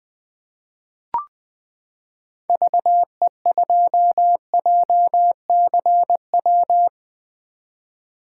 Morse code, heard as a keyed tone, about 20 wpm, 700 Hz. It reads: VE2JCW
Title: 2023 SOTA Chaser Call Signs - 20wpm